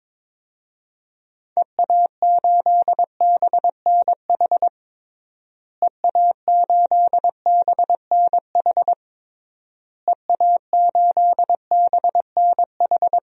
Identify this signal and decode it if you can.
Morse code, heard as a keyed tone, about 22 wpm, 700 Hz. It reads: EA8BNH EA8BNH EA8BNH
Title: DXCC-100 Call Signs - 6 Characters - Repeated 3x - 22wpm